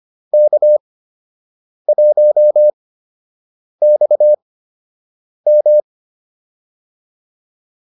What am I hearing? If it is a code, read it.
Morse code, heard as a keyed tone, about 25 wpm, spaced as for 7 wpm, 600 Hz. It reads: K1XM